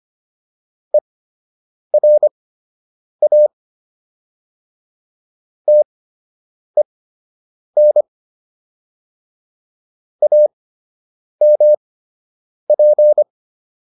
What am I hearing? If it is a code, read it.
Morse code, heard as a keyed tone, about 25 wpm, spaced as for 8 wpm, 600 Hz. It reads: ERA TEN AMP